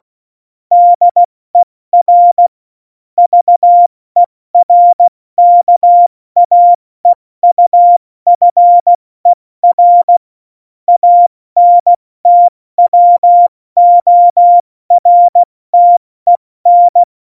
Morse code, heard as a keyed tone, about 16 wpm, 700 Hz. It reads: DER VERKAEUFER ANTWORTEN
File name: Morsekurs / Witze / Witz 20. November 2018, 70 F80